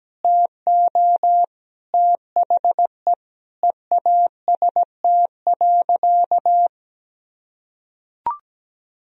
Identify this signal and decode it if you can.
Morse code, heard as a keyed tone, about 17 wpm, 700 Hz. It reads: TO THE EAST.
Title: Sentences from Top 1000 Words - Review Part 3 - 17wpm